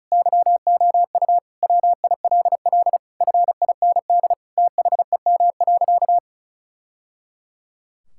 Morse code, heard as a keyed tone, about 34 wpm, 700 Hz. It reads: YOU WILL FIND THEM.
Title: Sentences from Top 1000 Words - Review Part 6 - 35wpm